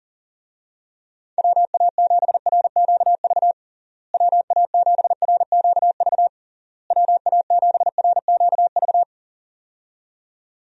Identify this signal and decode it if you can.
Morse code, heard as a keyed tone, about 37 wpm, 700 Hz. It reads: WA7RQV WA7RQV WA7RQV
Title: USA Call Signs - 6 Characters - Repeated 3x - 40wpm